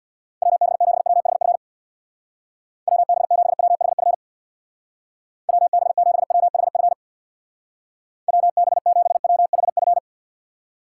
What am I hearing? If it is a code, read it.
Morse code, heard as a keyed tone, about 61 wpm, 700 Hz. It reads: WB7WHF WB7WHF WB7WHF WB7WHF
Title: Speed Racing - USA 2x3 Call Signs 50wpm